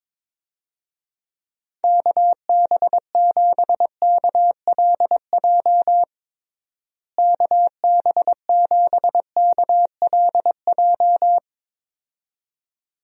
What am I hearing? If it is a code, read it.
Morse code, heard as a keyed tone, about 22 wpm, 700 Hz. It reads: KB7KLJ KB7KLJ
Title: USA Call Signs - 6 Characters - Repeated 2x - 22wpm